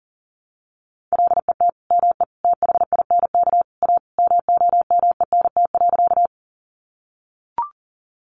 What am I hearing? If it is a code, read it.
Morse code, heard as a keyed tone, about 40 wpm, 700 Hz. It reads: LET ME THINK A MOMENT.
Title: Sentences from Top 1000 Words 40wpm